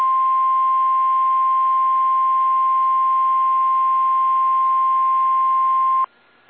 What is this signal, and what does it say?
Emergency Alert System (SAME) tones (the 1050 Hz attention tone). It is heard from 0.7 to 5.7 s.
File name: eas / KSDZKDJL / KSDZKDJL